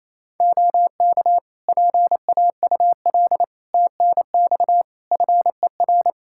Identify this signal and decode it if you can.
Morse code, heard as a keyed tone, about 27 wpm, 700 Hz. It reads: OK PAUL TNX FER
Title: K1USN SST QSO Practice 28wpm